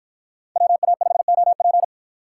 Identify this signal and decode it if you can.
Morse code, heard as a keyed tone, about 50 wpm, 700 Hz. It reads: WA5OP